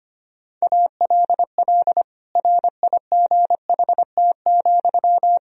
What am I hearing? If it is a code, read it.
Morse code, heard as a keyed tone, about 25 wpm, 700 Hz. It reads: ALL RIGHT,
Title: Sentences from Top 1000 Words - Review Part 4 - 25wpm